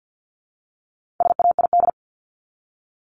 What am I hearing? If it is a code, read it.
Morse code, heard as a keyed tone, about 50 wpm, 700 Hz. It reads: SAID
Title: Top 500 Words Repeated 50wpm